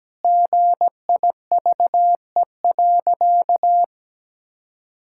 Morse code, heard as a keyed tone, about 17 wpm, 700 Hz. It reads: GIVE.
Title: Sentences from Top 1000 Words - Review Part 6 - 17wpm